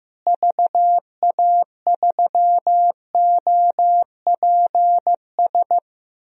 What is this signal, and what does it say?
Morse code, heard as a keyed tone, about 15 wpm, 700 Hz. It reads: VA3OPS
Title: DXCC-100 Call Signs - 6 Characters - Repeated 2x - 15wpm